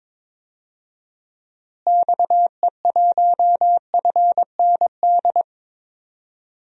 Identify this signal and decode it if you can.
Morse code, heard as a keyed tone, about 21 wpm, 700 Hz. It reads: XE1FND
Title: DXCC-100 Call Signs - 6 Characters - Repeated 3x - 22wpm